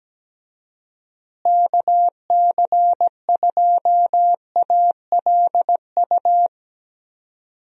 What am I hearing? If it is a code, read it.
Morse code, heard as a keyed tone, about 17 wpm, 700 Hz. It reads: KC2ALU
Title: USA Call Signs - 6 Characters - Repeated 2x - 17wpm